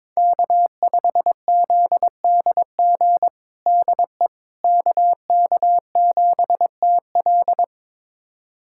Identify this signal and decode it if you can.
Morse code, heard as a keyed tone, about 22 wpm, 700 Hz. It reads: K5ZDG DE KK7TL
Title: USA Callsign DE Callsign - 5 Characters - 22wpm